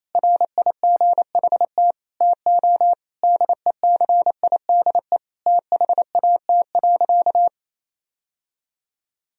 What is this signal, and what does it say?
Morse code, heard as a keyed tone, about 28 wpm, 700 Hz. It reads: RIGHT TO DECIDE THAT.
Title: Sentences from Top 1000 Words - Review Part 3 - 28wpm